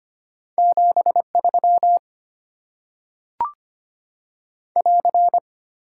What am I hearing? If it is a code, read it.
Morse code, heard as a keyed tone, about 25 wpm, 700 Hz. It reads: 73 <AR>
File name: Top 90 QSO Elements - 6x Character Spacing - 25wpm